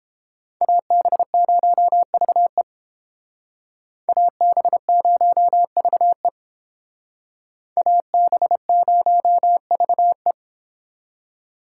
Morse code, heard as a keyed tone, about 30 wpm, 700 Hz. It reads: AB0VE AB0VE AB0VE